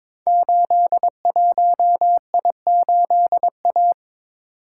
Morse code, heard as a keyed tone, about 22 wpm, 700 Hz. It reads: 81I8A
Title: Code Groups - 5 Letter:Numbers - 22wpm